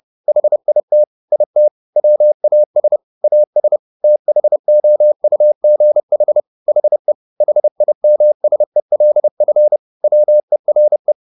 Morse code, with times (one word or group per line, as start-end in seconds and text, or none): 0.3-1.0 HIT
1.3-1.7 IT
2.0-3.0 WAS
3.2-3.8 AS
4.0-6.4 THOUGH
6.7-7.1 HE
7.4-9.8 HIMSELF
10.0-11.1 WERE